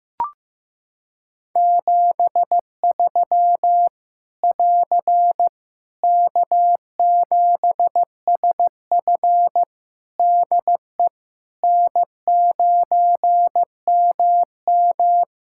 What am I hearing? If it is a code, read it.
Morse code, heard as a keyed tone, about 15 wpm, 700 Hz. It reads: 73 <AR> K7SF DE N9MM
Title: K1USN SST QSO Practice 15wpm